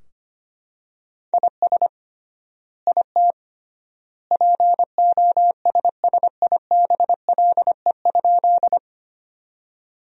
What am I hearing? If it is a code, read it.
Morse code, heard as a keyed tone, about 25 wpm, 700 Hz. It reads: IS IT POSSIBLE?
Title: 3x Word-Spacing - Sentences from Top 500 Words - 25wpm